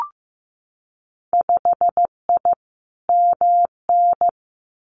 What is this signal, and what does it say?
Morse code, heard as a keyed tone, about 15 wpm, 700 Hz. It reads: 5I MN
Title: Winter Field Day Exchanges Encore 15wpm